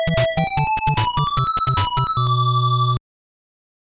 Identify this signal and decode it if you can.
There is no code.